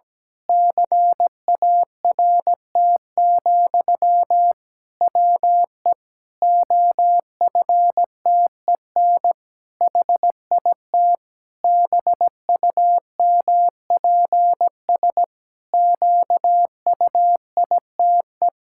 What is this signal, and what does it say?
Morse code, heard as a keyed tone, about 17 wpm, 700 Hz. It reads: CART, WE OFTEN HIT BUMPS QUITE
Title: Pangram Challenge v2 - 17wpm